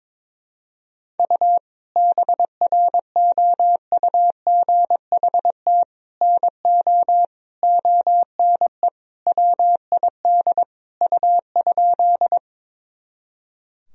Morse code, heard as a keyed tone, about 22 wpm, 700 Hz. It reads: U BROUGHT NO ONE WID U?
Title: Sentences from Top 500 Words and Abbreviations 22wpm